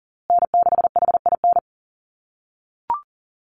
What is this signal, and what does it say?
Morse code, heard as a keyed tone, about 40 wpm, 700 Hz. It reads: N6HIN
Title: USA Call Signs Repeated 2x 40wpm